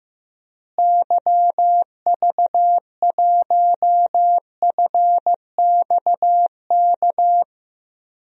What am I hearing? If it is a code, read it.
Morse code, heard as a keyed tone, about 15 wpm, 700 Hz. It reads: YV1FXK
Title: DXCC-100 Call Signs - 6 Characters - Repeated 3x - 15wpm